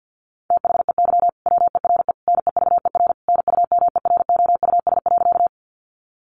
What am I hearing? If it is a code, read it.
Morse code, heard as a keyed tone, about 50 wpm, 700 Hz. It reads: THEY WERE NEVER NUMEROUS.
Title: Sentences from Top 1400 Words 50wpm